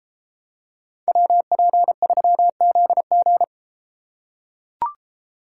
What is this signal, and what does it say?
Morse code, heard as a keyed tone, about 31 wpm, 700 Hz. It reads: WP3ZG